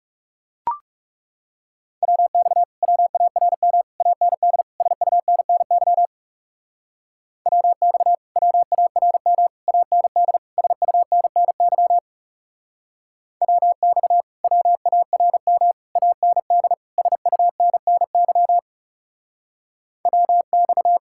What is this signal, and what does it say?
Morse code, heard as a keyed tone, about 38 wpm, 700 Hz. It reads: WX WARM AND SUNNY WX WARM AND SUNNY WX WARM AND SUNNY WX